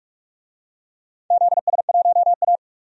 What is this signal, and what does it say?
Morse code, heard as a keyed tone, about 46 wpm, 700 Hz. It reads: ZS1A
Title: DXCC-100 Call Signs - 4 Characters - 45wpm